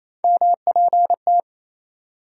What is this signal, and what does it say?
Morse code, heard as a keyed tone, about 28 wpm, 700 Hz. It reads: MPT
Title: USA Call Sign Postfixes - 3 Letters 28wpm